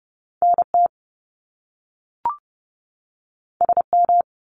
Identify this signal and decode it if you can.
Morse code, heard as a keyed tone, about 30 wpm, 700 Hz. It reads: NT SM
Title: Top Two Letter Combinations in Words 30wpm